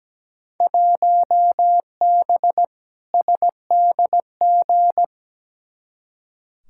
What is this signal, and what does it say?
Morse code, heard as a keyed tone, about 17 wpm, 700 Hz. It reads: 1B SDG
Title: Field Day Exchanges - 17wpm